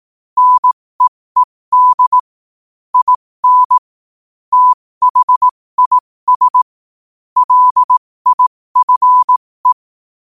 Morse code, heard as a keyed tone, about 18 wpm, spaced as for 15 wpm, 1000 Hz. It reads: NEED IN THIS LIFE
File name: qotdfiles / 15wpm